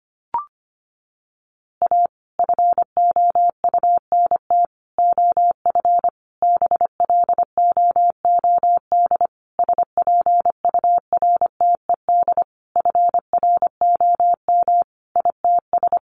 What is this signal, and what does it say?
Morse code, heard as a keyed tone, about 25 wpm, 700 Hz. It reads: A FOUNT OF BLOOD SPURTED FROM ITS
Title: A Sound Of Thunder 25wpm